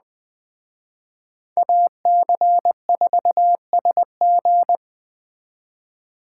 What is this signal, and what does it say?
Morse code, heard as a keyed tone, about 20 wpm, 700 Hz. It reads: AC4SG